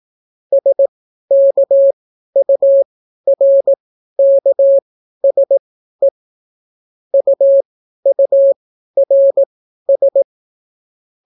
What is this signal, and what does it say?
Morse code, heard as a keyed tone, about 18 wpm, spaced as for 12 wpm, 550 Hz. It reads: SKURKSE UURS